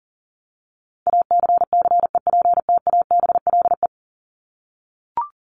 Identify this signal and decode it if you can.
Morse code, heard as a keyed tone, about 40 wpm, 700 Hz. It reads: ACCEPTABLE
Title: Top 1501-1750 Words 40wpm